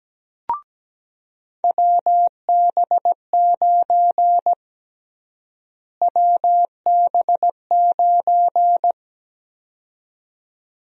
Morse code, heard as a keyed tone, about 17 wpm, 700 Hz. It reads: WB9 WB9
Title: USA Call Sign Prefixes - Repeated 2x - 17wpm